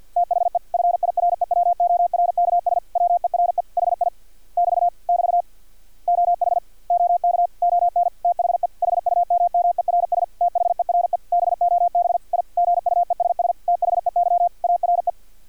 Morse code, heard as a keyed tone, about 50 wpm, 700 Hz. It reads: THE FIREWORKS WERE HI <BT> <BT> OH OK ON THE HUMMERS THERE BOB I GUESS THEY ARE
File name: archived / CRSnet